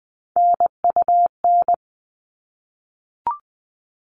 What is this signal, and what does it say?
Morse code, heard as a keyed tone, about 20 wpm, 700 Hz. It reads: NUN
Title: All Cut Numbers - 3-Digit 20wpm